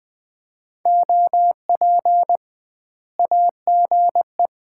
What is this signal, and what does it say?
Morse code, heard as a keyed tone, about 21 wpm, 700 Hz. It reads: OP AGE
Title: Common QSO Phrases - 2x Word-Spacing - 20wpm